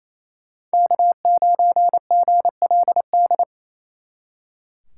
Morse code, heard as a keyed tone, about 33 wpm, 700 Hz. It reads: K9GLD
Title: Speed Racing - USA 1x3 Call Signs 28wpm